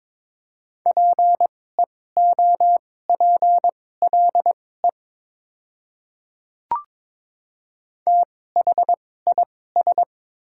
Morse code, heard as a keyed tone, about 22 wpm, spaced as for 16 wpm, 700 Hz. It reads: PEOPLE THIS
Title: 2x Character Spacing - Top 100 Words - 22wpm